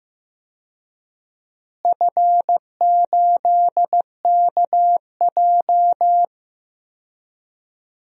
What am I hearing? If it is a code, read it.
Morse code, heard as a keyed tone, about 15 wpm, 700 Hz. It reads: F8KJ
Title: DXCC-100 Call Signs - 4 Characters - Repeated 3x - 15wpm